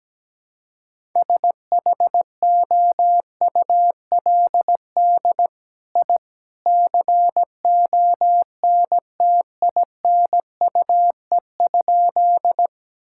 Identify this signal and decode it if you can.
Morse code, heard as a keyed tone, about 17 wpm, 700 Hz. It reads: SHOULD I CONTINUE?
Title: Sentences from Top 1000 Words - Review Part 6 - 17wpm